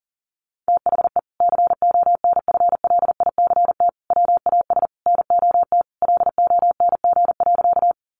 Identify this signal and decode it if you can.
Morse code, heard as a keyed tone, about 37 wpm, 700 Hz. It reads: THE CONFLICT WAS NOT LONG.
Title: Sentences from Top 1300 Words 40wpm